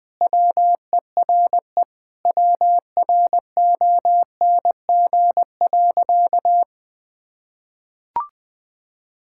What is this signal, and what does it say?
Morse code, heard as a keyed tone, about 20 wpm, 700 Hz. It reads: WERE WRONG.
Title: Sentences with 5-Word Phrases - Spoken Beginning - 20wpm